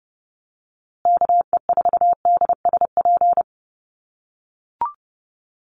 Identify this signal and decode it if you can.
Morse code, heard as a keyed tone, about 30 wpm, 700 Hz. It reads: KE4DSP
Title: USA Call Signs Repeated 3x 30wpm